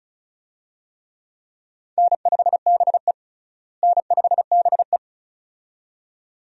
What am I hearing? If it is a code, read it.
Morse code, heard as a keyed tone, about 36 wpm, 700 Hz. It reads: N5BE N5BE
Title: USA Call Signs - 4 Characters - Repeated 2x - 35wpm